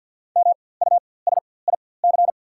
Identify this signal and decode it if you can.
Morse code, heard as a keyed tone, about 48 wpm, spaced as for 23 wpm, 690 Hz. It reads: MUSIC
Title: 4x Character Spacing - Top 500 Words - 50wpm